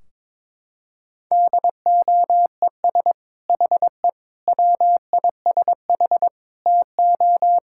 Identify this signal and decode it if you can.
Morse code, heard as a keyed tone, about 23 wpm, 700 Hz. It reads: DOES HE WISH TO
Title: Sentences from Top 1600 Words 22wpm